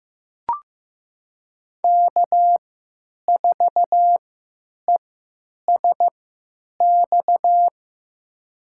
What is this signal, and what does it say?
Morse code, heard as a keyed tone, about 15 wpm, spaced as for 9 wpm, 700 Hz. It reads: K4ESX
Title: USA Call Signs - 5 Characters - 3x Character Spacing - 15wpm